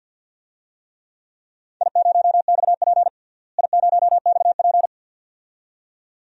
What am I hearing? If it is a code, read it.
Morse code, heard as a keyed tone, about 47 wpm, 700 Hz. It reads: I0XP I0XP
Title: DXCC-100 Call Signs - 4 Characters - Repeated 2x - 50wpm